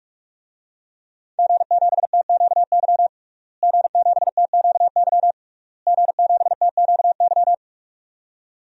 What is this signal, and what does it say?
Morse code, heard as a keyed tone, about 46 wpm, 700 Hz. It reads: G7TQY G7TQY G7TQY
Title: DXCC-100 Call Signs - 5 Characters - Repeated 3x - 45wpm